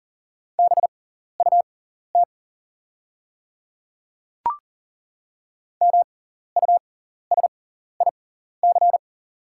Morse code, heard as a keyed tone, about 40 wpm, spaced as for 14 wpm, 700 Hz. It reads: BUT MUSIC
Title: 6x Character Spacing - Top 500 Words - 40wpm